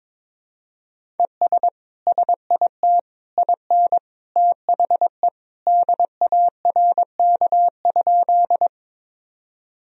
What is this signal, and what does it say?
Morse code, heard as a keyed tone, about 22 wpm, 700 Hz. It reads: ES SIT IN THE DARK?
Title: Sentences from Top 500 Words and Abbreviations 22wpm